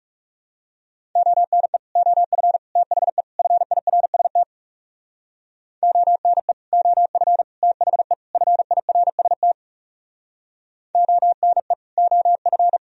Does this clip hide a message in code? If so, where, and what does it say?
Yes: Morse code from 1.2 to 12.8 s, heard as a keyed tone, about 41 wpm, 700 Hz. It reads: ONE OF THE FIRST ONE OF THE FIRST ONE OF